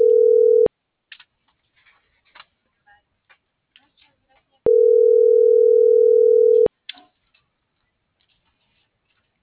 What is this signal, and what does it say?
Ringback tone (North America), 440 + 480 Hz.